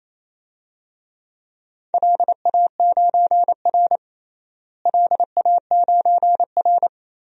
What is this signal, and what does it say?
Morse code, heard as a keyed tone, about 28 wpm, 700 Hz. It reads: LA9R LA9R
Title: DXCC-100 Call Signs - 4 Characters - Repeated 2x - 28wpm